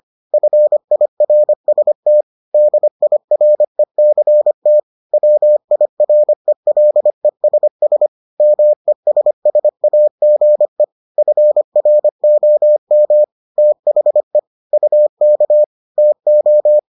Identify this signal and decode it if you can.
Morse code, heard as a keyed tone, about 25 wpm, 600 Hz. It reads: FIRST DIRECT WIRELESS MESSAGE FROM THE UK TO